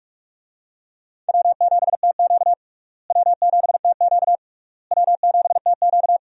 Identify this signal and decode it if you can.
Morse code, heard as a keyed tone, about 42 wpm, 700 Hz. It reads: W7TQ W7TQ W7TQ
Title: USA Call Signs - 4 Characters - Repeated 3x - 45wpm